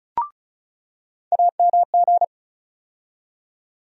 Morse code, heard as a keyed tone, about 36 wpm, 700 Hz. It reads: AMG
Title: USA Call Sign Postfixes - 3 Letters 35wpm